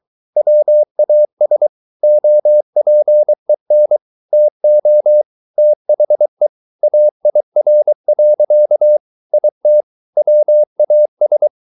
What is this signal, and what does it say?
Morse code, heard as a keyed tone, about 23 wpm, 600 Hz. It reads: WAS OPEN TO THE AIR. IT WAS